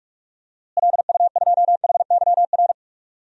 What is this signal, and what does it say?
Morse code, heard as a keyed tone, about 42 wpm, 700 Hz. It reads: LU2HYR
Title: DXCC-100 Call Signs - 6 Characters - Repeated 3x - 45wpm